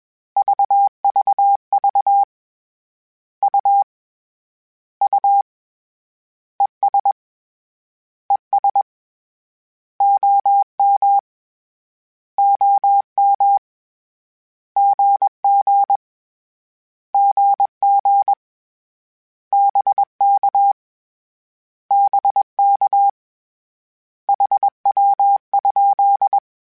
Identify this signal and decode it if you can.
Morse code, heard as a keyed tone, about 21 wpm, 800 Hz. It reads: VVV U U ES ES OM OM GG GG BK BK HW?
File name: Abreviations / Abreviations mp3s / Abreviations@20